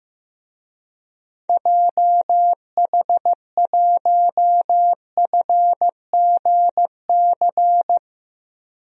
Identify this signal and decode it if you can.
Morse code, heard as a keyed tone, about 15 wpm, 700 Hz. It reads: JH1FGC
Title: DXCC-100 Call Signs - 6 Characters - Repeated 3x - 15wpm